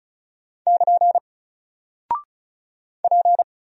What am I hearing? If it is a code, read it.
Morse code, heard as a keyed tone, about 36 wpm, 700 Hz. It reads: <KN> P
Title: Single Character (Letter:Number:Punctuation:Prosign) 35wpm